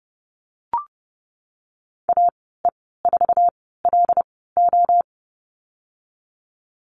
Morse code, heard as a keyed tone, about 30 wpm, spaced as for 17 wpm, 700 Hz. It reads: AE4LO
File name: Lesson 14g - Number 2 - Call Signs 30wpm